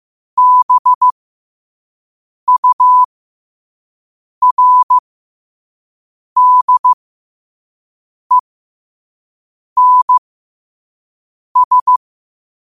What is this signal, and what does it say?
Morse code, heard as a keyed tone, about 15 wpm, spaced as for 5 wpm, 1000 Hz. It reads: BURDENS
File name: qotdfiles / 5wpm